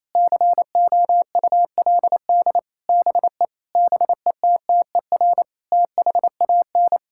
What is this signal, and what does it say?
Morse code, heard as a keyed tone, about 28 wpm, 700 Hz. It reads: COULD BE BETTER THAN
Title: Sentences from Top 1000 Words - Review Part 6 - 28wpm